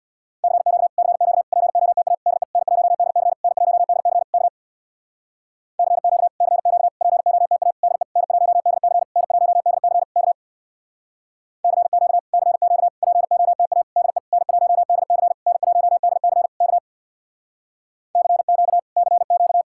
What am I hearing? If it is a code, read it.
Morse code, heard as a keyed tone, about 66 wpm, 700 Hz. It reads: CQ CQ POTA DE N1DC N1DC K CQ CQ POTA DE N1DC N1DC K CQ CQ POTA DE N1DC N1DC K CQ CQ